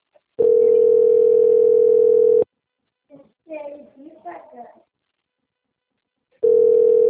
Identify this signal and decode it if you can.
Ringback tone (North America), 440 + 480 Hz, with speech.
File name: out- - - - - .23997